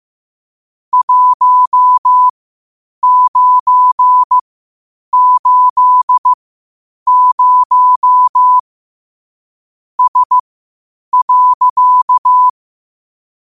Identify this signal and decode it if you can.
Morse code, heard as a keyed tone, about 15 wpm, spaced as for 8 wpm, 1000 Hz. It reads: 1980 S.
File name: qotdfiles / 8wpm